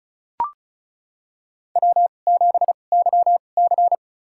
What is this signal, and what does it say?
Morse code, heard as a keyed tone, about 36 wpm, spaced as for 25 wpm, 700 Hz. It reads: W7YC
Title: USA Call Signs - 4 Characters - 2x Character Spacing - 35wpm